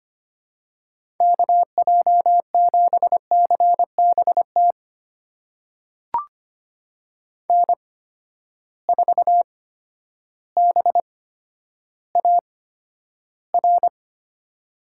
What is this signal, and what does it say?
Morse code, heard as a keyed tone, about 25 wpm, spaced as for 11 wpm, 700 Hz. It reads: KJ7CBT N4BAR
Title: USA Call Signs - 8x Character Spacing - 25wpm